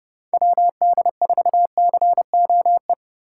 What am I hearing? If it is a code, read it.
Morse code, heard as a keyed tone, about 29 wpm, 700 Hz. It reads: WD4COE